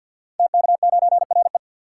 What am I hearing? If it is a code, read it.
Morse code, heard as a keyed tone, about 53 wpm, 700 Hz. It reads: TK9RE
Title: Code Groups - 5 Letter:Numbers - 50wpm